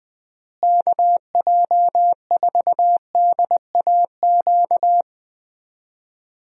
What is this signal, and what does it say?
Morse code, heard as a keyed tone, about 20 wpm, 700 Hz. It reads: KJ4DAQ
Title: USA Call Signs - 6 Characters - Repeated 3x - 20wpm